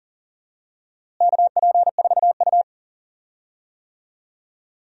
Morse code, heard as a keyed tone, about 40 wpm, 700 Hz. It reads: KP4U
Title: DXCC-100 Call Signs - 4 Characters - Repeated 2x - 40wpm